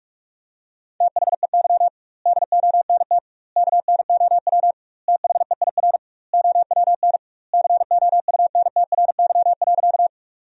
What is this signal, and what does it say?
Morse code, heard as a keyed tone, about 46 wpm, 700 Hz. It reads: THEY DONT KNOW THEIR OWN COUNTRY.
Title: 2x Word-Spacing - Sentences from Top 300 Words - 45wpm